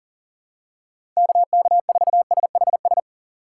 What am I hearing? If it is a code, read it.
Morse code, heard as a keyed tone, about 38 wpm, 700 Hz. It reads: KK4SHS
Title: USA Call Signs - 6 Characters - 2x Character Spacing - 40wpm